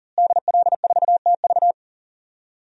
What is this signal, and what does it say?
Morse code, heard as a keyed tone, about 40 wpm, 700 Hz. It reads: DL4TV
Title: DXCC-100 Call Signs - 5 Characters - Repeated 3x - 40wpm